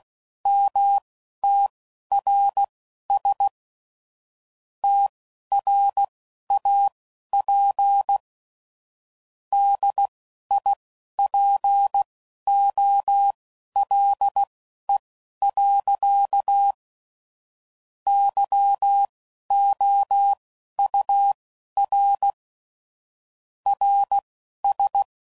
Morse code, heard as a keyed tone, about 16 wpm, spaced as for 11 wpm, 800 Hz. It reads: MTRS TRAP DIPOLE. YOUR RS
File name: CW Code Groups @15WPM/QSOs/WD0FBT Spaced